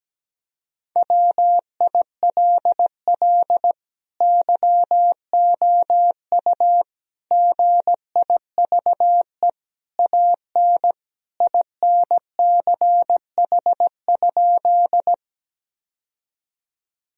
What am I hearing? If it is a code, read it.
Morse code, heard as a keyed tone, about 17 wpm, 700 Hz. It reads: WILL YOU GIVE AN INCH?
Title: Sentences from Top 1000 Words - Review Part 3 - 17wpm